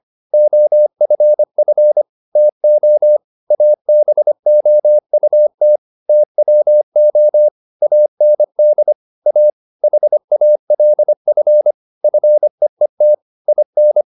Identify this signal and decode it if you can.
Morse code, heard as a keyed tone, about 25 wpm, 600 Hz. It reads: OFF TO ABOUT TWO AND A HALF FEET IN